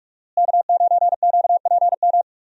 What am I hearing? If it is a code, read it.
Morse code, heard as a keyed tone, about 46 wpm, 700 Hz. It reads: K9QPM
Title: USA Call Signs - 5 Characters - Repeated 2x - 45wpm